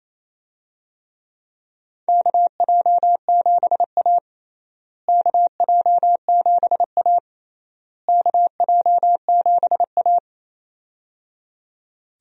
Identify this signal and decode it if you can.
Morse code, heard as a keyed tone, about 28 wpm, 700 Hz. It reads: KJ7A KJ7A KJ7A
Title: USA Call Signs - 4 Characters - Repeated 3x - 28wpm